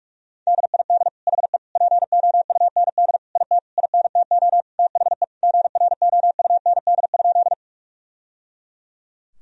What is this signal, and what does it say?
Morse code, heard as a keyed tone, about 46 wpm, 700 Hz. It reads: DID HE POUND IT INTO THE GROUND?
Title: Sentences from Top 1000 Words - Review Part 2 - 45wpm